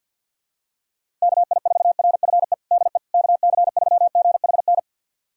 Morse code, heard as a keyed tone, about 47 wpm, 700 Hz. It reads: KI4RFE DE KC3GHN
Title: USA Callsign DE Callsign - 6 Characters - 50wpm